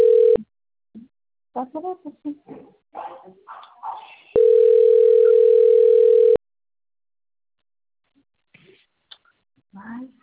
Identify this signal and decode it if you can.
Ringback tone (North America), 440 + 480 Hz.